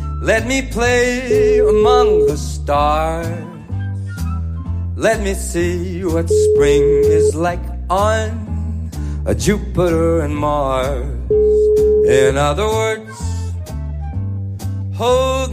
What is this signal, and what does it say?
Ringback tone (Europe, 425 Hz), with speech.